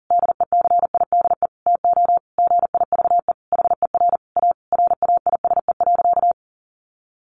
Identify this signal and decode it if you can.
Morse code, heard as a keyed tone, about 37 wpm, 700 Hz. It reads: DECIDE TO GIVE HER A RAISE.
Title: Easy Phrases 40wpm